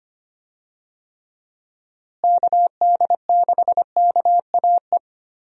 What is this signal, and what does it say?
Morse code, heard as a keyed tone, about 25 wpm, 700 Hz. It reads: KD6KAE
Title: USA Call Signs - 6 Characters - Repeated 2x - 25wpm